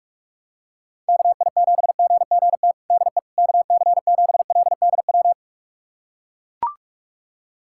Morse code, heard as a keyed tone, about 41 wpm, 700 Hz. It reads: KI7GGT DE KC7LDW
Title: USA Callsign DE Callsign - 6 Characters - 45wpm